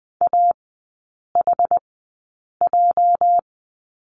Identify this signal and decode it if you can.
Morse code, heard as a keyed tone, about 20 wpm, spaced as for 8 wpm, 700 Hz. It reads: AHJ
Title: 205 Lesson 33e - Letter j - Three Characters 20wpm